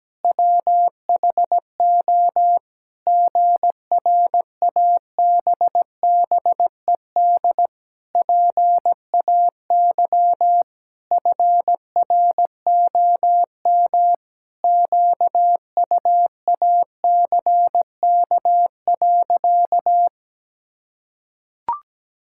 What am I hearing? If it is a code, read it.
Morse code, heard as a keyed tone, about 17 wpm, 700 Hz. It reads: WHO GRABBED PAY FROM QUACK.
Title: Pangram Challenge v2 - 17wpm